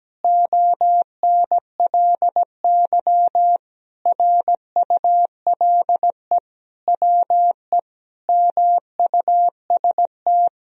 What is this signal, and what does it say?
Morse code, heard as a keyed tone, about 17 wpm, 700 Hz. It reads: ONLY RULE WE MUST
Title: Sentences from Top 1000 Words - Review Part 2 - 17wpm